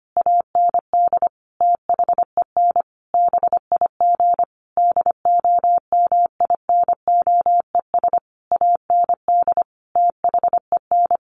Morse code, heard as a keyed tone, about 25 wpm, 700 Hz. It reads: AND THEN BIG DOMINOES AND THEN
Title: A Sound Of Thunder 25wpm